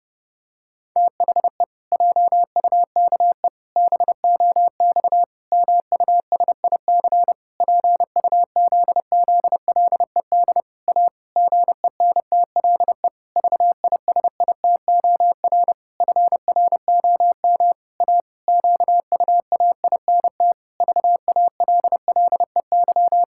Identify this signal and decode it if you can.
Morse code, heard as a keyed tone, about 28 wpm, 700 Hz. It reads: THE JUKE BOX MUSIC PUZZLED A GENTLE VISITOR FROM A QUAINT VALLEY
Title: Pangram Challenge v2 - 30wpm